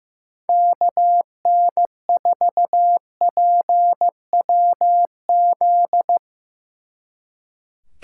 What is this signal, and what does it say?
Morse code, heard as a keyed tone, about 16 wpm, 700 Hz. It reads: KN4PWZ
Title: Speed Racing - USA Call Signs 15wpm